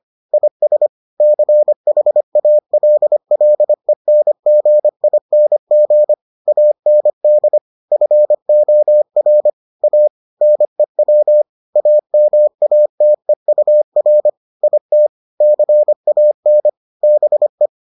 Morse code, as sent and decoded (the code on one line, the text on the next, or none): .. ... / -.-. .... .- .-.. .-.. -. --. .. -. --. / .- -. -.. / ..-. --- .-. / .- / -. .-- / .- -- .- - . ..- .-. / .. - / -.-. .- -. / -... .
IS CHALLENGING AND FOR A NEW AMATEUR IT CAN BE